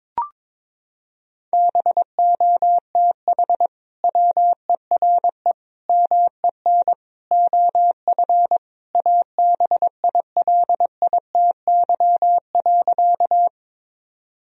Morse code, heard as a keyed tone, about 23 wpm, 700 Hz. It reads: BOTH WERE MEN OF ABILITY.
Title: Sentences from Top 1000 Words - Review Part 3 - 22wpm